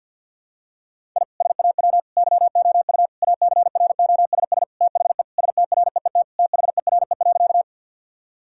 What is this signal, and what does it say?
Morse code, heard as a keyed tone, about 52 wpm, 700 Hz. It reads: I SAW YOU ACROSS THE STREET THERE.
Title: Sentences from Top 1000 Words - Review Part 5 - 50wpm